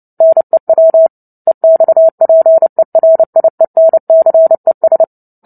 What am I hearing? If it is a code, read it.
Morse code, heard as a keyed tone, about 30 wpm, 650 Hz. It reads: NEW EXPERIENCES
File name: Book Two / 03-Chapter 3 - 29 WPM